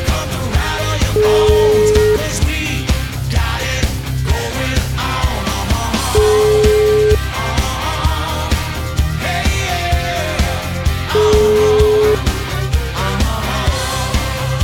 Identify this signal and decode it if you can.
Ringback tone (Europe, 425 Hz), with noise.